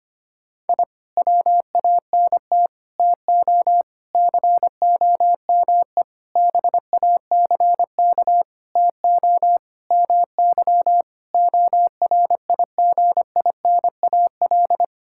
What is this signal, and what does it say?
Morse code, heard as a keyed tone, about 25 wpm, 700 Hz. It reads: I WANT TO COME BACK TO MY ORIGINAL
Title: Sentences from Top 1000 Words - Review Part 6 - 25wpm